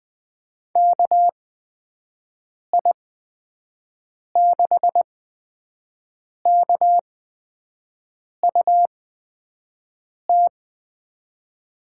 Morse code, heard as a keyed tone, about 20 wpm, spaced as for 5 wpm, 700 Hz. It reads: KI6KUT
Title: USA Call Signs - 6 Characters - 8x Character Spacing - 20wpm